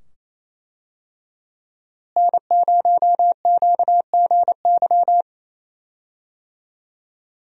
Morse code, heard as a keyed tone, about 28 wpm, 700 Hz. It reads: N0QGY
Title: USA Call Signs - 5 Characters - Repeated 2x - 28wpm